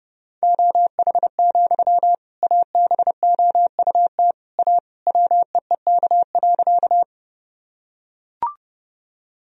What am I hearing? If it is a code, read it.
Morse code, heard as a keyed tone, about 30 wpm, 700 Hz. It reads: OH, ABOUT A WEEK.
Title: Sentences from Top 1000 Words - Review Part 2 - 30wpm